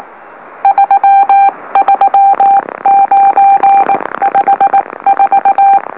Morse code, heard as a keyed tone, about 19 wpm, spaced as for 16 wpm, 790 Hz. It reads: 33954
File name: M04- -2345utc-LOLO